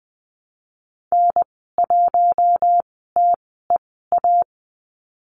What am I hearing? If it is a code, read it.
Morse code, heard as a keyed tone, about 20 wpm, spaced as for 14 wpm, 700 Hz. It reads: N1TEA
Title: 082 Lesson 14g - Number 2 - Call Signs 20wpm